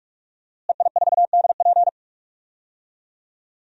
Morse code, heard as a keyed tone, about 46 wpm, 700 Hz. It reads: EI4DP
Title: DXCC-100 Call Signs - 5 Characters - Repeated 2x - 45wpm